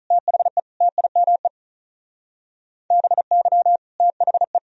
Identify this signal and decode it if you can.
Morse code, heard as a keyed tone, about 43 wpm, 700 Hz. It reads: THE TIME BY THE
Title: Speed-Racing - Sets of 3 Words 35wpm